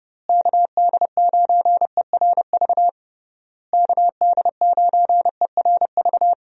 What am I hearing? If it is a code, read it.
Morse code, heard as a keyed tone, about 30 wpm, 700 Hz. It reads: KD9ERV KD9ERV
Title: USA Call Signs - 6 Characters - Repeated 3x - 30wpm